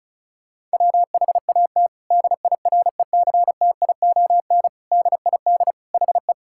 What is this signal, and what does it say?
Morse code, heard as a keyed tone, about 36 wpm, 700 Hz. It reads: WHAT DIRECTION DID HE
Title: Sentences from Top 1000 Words - Review Part 5 - 35wpm